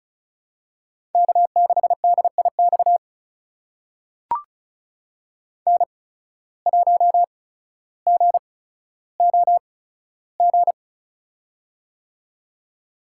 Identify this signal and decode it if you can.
Morse code, heard as a keyed tone, about 36 wpm, spaced as for 15 wpm, 700 Hz. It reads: K6DIX N1GOG
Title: USA Call Signs - 5 Characters - 8x Character Spacing - 35wpm